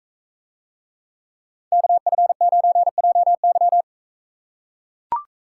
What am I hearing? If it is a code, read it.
Morse code, heard as a keyed tone, about 44 wpm, 700 Hz. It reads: KF9JY